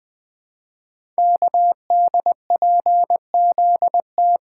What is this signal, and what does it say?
Morse code, heard as a keyed tone, about 20 wpm, 700 Hz. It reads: KDPZT
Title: Code Groups - 5 Letters - 20wpm